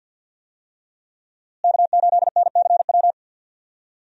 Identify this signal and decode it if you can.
Morse code, heard as a keyed tone, about 50 wpm, 700 Hz. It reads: K8NCW
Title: USA Call Signs - 5 Characters - Repeated 2x - 50wpm